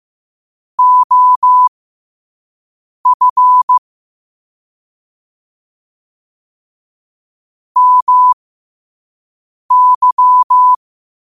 Morse code, heard as a keyed tone, about 15 wpm, spaced as for 5 wpm, 1000 Hz. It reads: OF MY
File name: qotdfiles / 5wpm